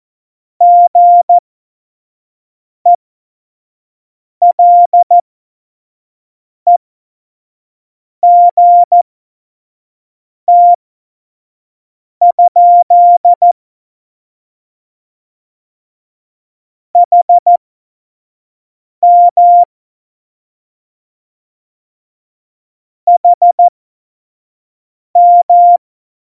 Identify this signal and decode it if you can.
Morse code, heard as a keyed tone, about 14 wpm, spaced as for 5 wpm, 700 Hz. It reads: GELEGT? HM HM
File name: Morsekurs / Witze / Witz 29. August 17, Tempo 25 F70